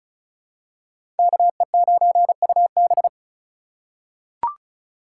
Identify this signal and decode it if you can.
Morse code, heard as a keyed tone, about 34 wpm, 700 Hz. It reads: KE9UB
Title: USA Call Signs - 5 Characters - 3x Character Spacing - 35wpm